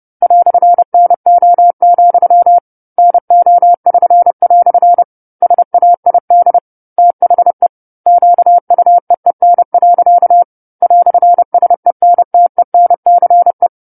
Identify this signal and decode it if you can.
Morse code, heard as a keyed tone, about 30 wpm, 700 Hz. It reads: "NO, NO<SN>" SAID THE QUEEN. "SENTENCE
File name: 10-Chapter-10 30 WPM